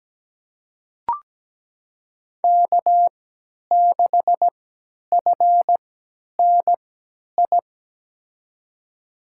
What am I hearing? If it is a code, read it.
Morse code, heard as a keyed tone, about 17 wpm, spaced as for 10 wpm, 700 Hz. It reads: K6FNI
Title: USA Call Signs - 3x Character Spacing - 17wpm